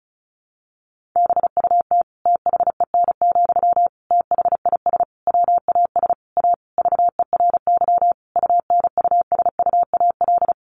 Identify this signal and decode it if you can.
Morse code, heard as a keyed tone, about 34 wpm, 700 Hz. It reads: BUT THEN, THIS WAS A VERY UNUSUAL
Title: Sentences from Top 1400 Words 35wpm